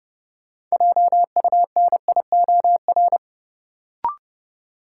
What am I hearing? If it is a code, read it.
Morse code, heard as a keyed tone, about 30 wpm, 700 Hz. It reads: JUNIOR
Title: Sentences from Top 1000 Words - Review Part 7 - New Word Drill - 30wpm